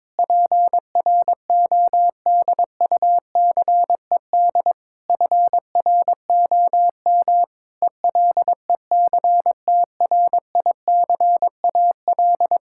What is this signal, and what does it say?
Morse code, heard as a keyed tone, about 22 wpm, 700 Hz. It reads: PRODUCED FROM ELECTRICAL
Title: Sentences from Top 1000 Words - Review Part 6 - 22wpm